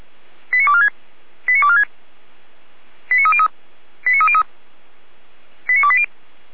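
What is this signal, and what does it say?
Five-tone selective call (ZVEI), heard as quick runs of five tones, tones of about 70 ms. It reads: 89246 89246 89303 89303 89280
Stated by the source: ZVEI five-tone selective call received on an SDR.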